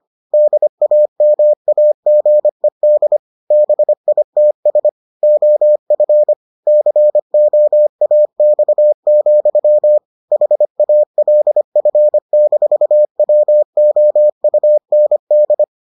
Morse code, heard as a keyed tone, about 25 wpm, 600 Hz. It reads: DAMAGED BITS OF COAX, HALF-WOUND